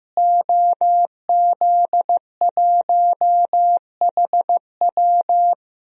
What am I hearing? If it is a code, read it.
Morse code, heard as a keyed tone, about 15 wpm, 700 Hz. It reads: OZ1HW
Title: DXCC-100 Call Signs - 5 Characters - Repeated 3x - 15wpm